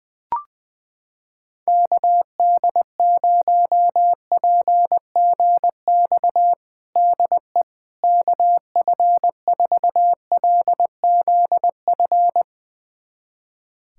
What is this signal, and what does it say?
Morse code, heard as a keyed tone, about 20 wpm, 700 Hz. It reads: KD0PGX DE KF4LZF
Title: USA Callsign DE Callsign 20wpm